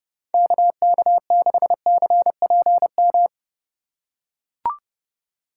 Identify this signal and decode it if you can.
Morse code, heard as a keyed tone, about 32 wpm, 700 Hz. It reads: KK6CPM
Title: USA Call Signs - 6 Characters - 2x Character Spacing - 30wpm